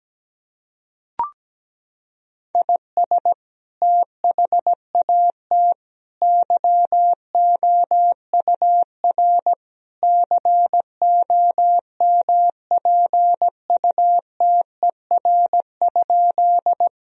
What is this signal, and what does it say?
Morse code, heard as a keyed tone, about 17 wpm, 700 Hz. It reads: IS THAT YOUR COMPUTER?
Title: Sentences from Top 1000 Words - Review Part 3 - 17wpm